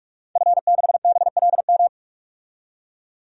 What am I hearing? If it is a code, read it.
Morse code, heard as a keyed tone, about 46 wpm, 700 Hz. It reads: F6BLM
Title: DXCC-100 Call Signs - 5 Characters - 45wpm